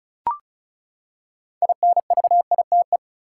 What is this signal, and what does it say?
Morse code, heard as a keyed tone, about 34 wpm, 700 Hz. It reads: INVITE
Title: Six Letter Words - 35wpm